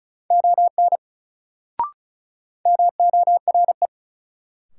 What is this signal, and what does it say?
Morse code, heard as a keyed tone, about 36 wpm, 700 Hz. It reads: ON MORE
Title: Sentences from Top 100 Words - New Word Drill - 35wpm